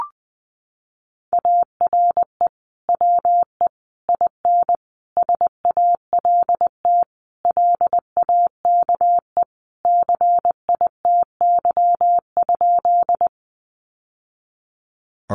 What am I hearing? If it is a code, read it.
Morse code, heard as a keyed tone, about 20 wpm, 700 Hz. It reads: ARE WE IN SALT LAKE CITY?
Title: Sentences from Top 1100 Words 20wpm